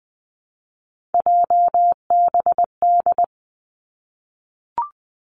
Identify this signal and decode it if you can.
Morse code, heard as a keyed tone, about 20 wpm, 700 Hz. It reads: JBD
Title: Three Challenging Letter-Number Combinations 20wpm